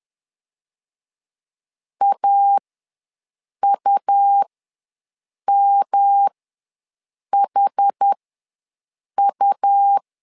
Morse code, heard as a keyed tone, about 10 wpm, spaced as for 6 wpm, 800 Hz. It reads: AUMHU
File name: LEZIONI / 3 MWRO / 3 M ws 10wpm 2121